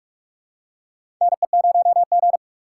Morse code, heard as a keyed tone, about 42 wpm, 700 Hz. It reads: NE0G
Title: USA Call Signs - 4 Characters - 6x Character Spacing - 45wpm